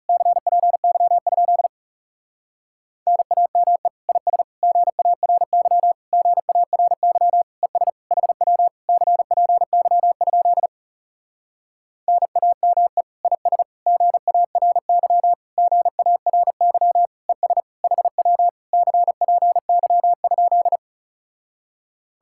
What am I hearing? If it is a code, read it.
Morse code, heard as a keyed tone, about 37 wpm, 700 Hz. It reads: CPY? NAME IS GARY GARY ES HW CPY? NAME IS GARY GARY ES HW CPY?
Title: Speed-Racing Common QSO Phrases - 30wpm